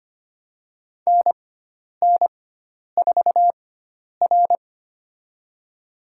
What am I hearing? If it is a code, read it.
Morse code, heard as a keyed tone, about 25 wpm, spaced as for 10 wpm, 700 Hz. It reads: NN4R
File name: Lesson 16g - Letter c - Call Signs 25wpm